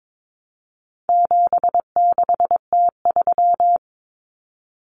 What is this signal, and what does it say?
Morse code, heard as a keyed tone, about 22 wpm, 700 Hz. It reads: 76T3
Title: Common Cut Numbers - 4-Digit 22wpm